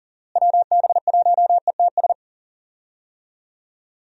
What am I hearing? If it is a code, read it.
Morse code, heard as a keyed tone, about 40 wpm, 700 Hz. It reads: WB1ETS
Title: USA Call Signs - 6 Characters - Repeated 2x - 40wpm